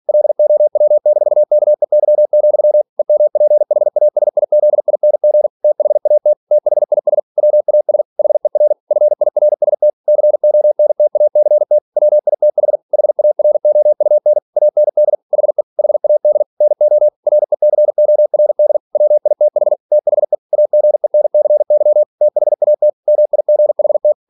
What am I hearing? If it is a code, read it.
Morse code, heard as a keyed tone, about 46 wpm, 600 Hz. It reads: LOW-KEY, EMPHASIZING THAT THIS WAS HER FIRST CONTACT WITH HAROUN AND HE HAD NO RECORD WITH THE AGENCY THAT MIGHT
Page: Never 47wpm